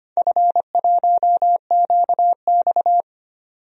Morse code, heard as a keyed tone, about 25 wpm, 690 Hz. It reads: F1QX
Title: DXCC-100 Call Signs - 4 Characters - 25wpm